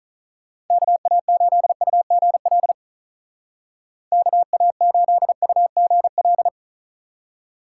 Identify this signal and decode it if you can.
Morse code, heard as a keyed tone, about 47 wpm, 700 Hz. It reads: KA8UGL KA8UGL